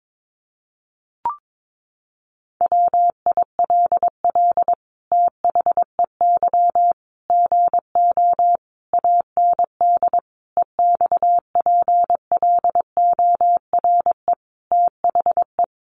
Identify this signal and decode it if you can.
Morse code, heard as a keyed tone, about 22 wpm, 700 Hz. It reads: WILL THEY GO AND EXPLORE THE
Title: Sentences from Top 1100 Words 22wpm